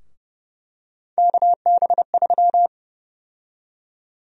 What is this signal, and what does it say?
Morse code, heard as a keyed tone, about 30 wpm, 700 Hz. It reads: KB3
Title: USA Call Sign Prefixes - Repeated 2x - 30wpm